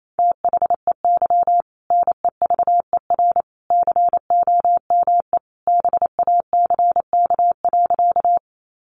Morse code, heard as a keyed tone, about 28 wpm, 700 Hz. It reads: THEY NEVER COME BACK.
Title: Sentences from Top 300 Words 28wpm